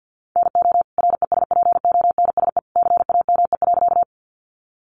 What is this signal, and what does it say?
Morse code, heard as a keyed tone, about 50 wpm, 700 Hz. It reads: NO RESPONSE CAME.
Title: Sentences from Top 1400 Words 50wpm